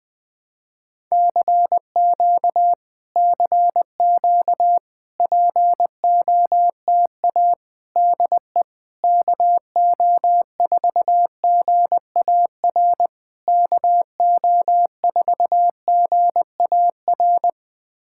Morse code, heard as a keyed tone, about 20 wpm, 700 Hz. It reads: CQ CQ POTA DE KO4GAR KO4GAR
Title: POTA QSO Practice - 20wpm